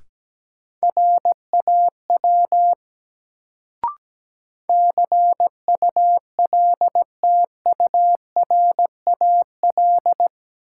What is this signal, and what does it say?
Morse code, heard as a keyed tone, about 17 wpm, 700 Hz. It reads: RAW CULTURAL
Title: Sentences from Top 1500 Words - New Word Drill Alt - 17wpm